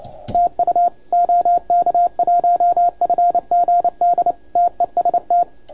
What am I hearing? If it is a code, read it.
Morse code, heard as a keyed tone, about 28 wpm, 690 Hz, with radio noise. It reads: TU OK1FGD TEST
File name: Archiv / CUC287 / OK1FGD CUC 287